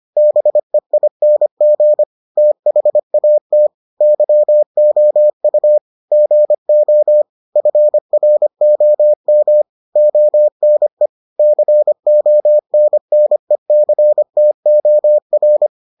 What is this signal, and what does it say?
Morse code, heard as a keyed tone, about 25 wpm, 600 Hz. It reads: BEING THAT YOU GO FROM ONE CONNECTOR